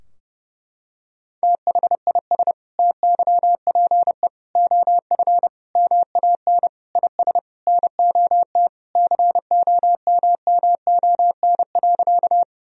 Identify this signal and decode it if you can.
Morse code, heard as a keyed tone, about 30 wpm, 700 Hz. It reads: THIS TYPE OF MAN IS NOT COMMON.
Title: Sentences from Top 1000 Words - Review Part 3 - 30wpm